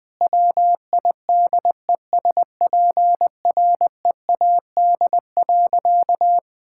Morse code, heard as a keyed tone, about 20 wpm, 700 Hz. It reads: WIDESPREAD.
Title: Sentences with 3-Word Phrases - 20wpm